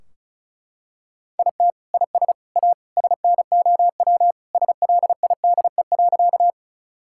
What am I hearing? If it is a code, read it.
Morse code, heard as a keyed tone, about 34 wpm, 700 Hz. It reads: IT IS A SNOW SLIDE.
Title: Sentences from Top 1600 Words 35wpm